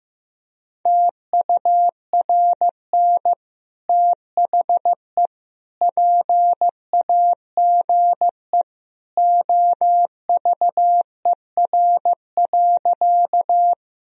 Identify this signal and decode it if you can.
Morse code, heard as a keyed tone, about 15 wpm, 700 Hz. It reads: TURN THE PAGE OVER.
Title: Sentences from Top 1000 Words - Review Part 1 - 15wpm